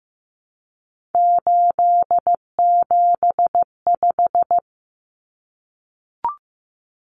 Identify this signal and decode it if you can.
Morse code, heard as a keyed tone, about 15 wpm, 700 Hz. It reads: 875
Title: Numbers - 3-Digit 15wpm